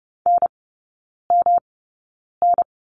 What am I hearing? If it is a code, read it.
Morse code, heard as a keyed tone, about 30 wpm, spaced as for 9 wpm, 700 Hz. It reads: NMN
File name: Lesson 18e - Letter m - Three Characters 30wpm